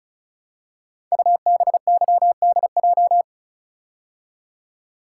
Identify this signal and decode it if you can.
Morse code, heard as a keyed tone, about 36 wpm, 690 Hz. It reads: UBYDJ
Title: Code Groups - 5 Letters - 35wpm